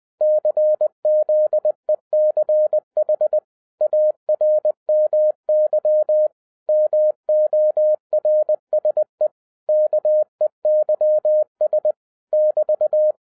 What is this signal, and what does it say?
Morse code, heard as a keyed tone, about 20 wpm, 610 Hz. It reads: CZECH ARMY MORSE KEYS <BT>
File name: 2011-07-mp3 / n43 Czech Army Morse Keys 20wpm